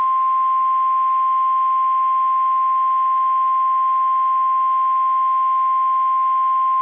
Emergency Alert System (SAME) tones (the 1050 Hz attention tone). It is heard from 0.6 to 6.5 s.
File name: eas / KSDZKDJL / KSDZKDJL